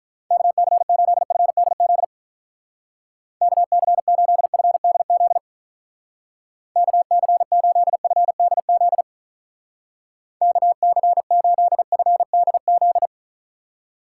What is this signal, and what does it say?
Morse code, heard as a keyed tone, about 45 wpm, 700 Hz. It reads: KC8FDZ KC8FDZ KC8FDZ KC8FDZ